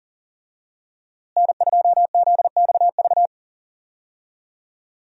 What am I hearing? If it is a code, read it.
Morse code, heard as a keyed tone, about 37 wpm, 700 Hz. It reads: N2ZXV
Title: USA Call Signs - 5 Characters - Repeated 3x - 40wpm